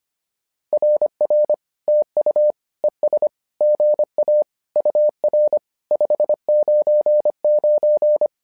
Morse code, heard as a keyed tone, about 25 wpm, 600 Hz. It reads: RR TU ES GA UR 599
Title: SOTA QSO Practice 25wpm